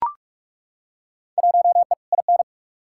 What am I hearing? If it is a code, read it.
Morse code, heard as a keyed tone, about 41 wpm, 700 Hz. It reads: 1E IN
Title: Field Day Exchanges - 45wpm